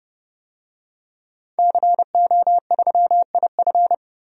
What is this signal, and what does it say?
Morse code, heard as a keyed tone, about 30 wpm, 700 Hz. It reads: CO3IF
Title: DXCC-100 Call Signs - 5 Characters - 30wpm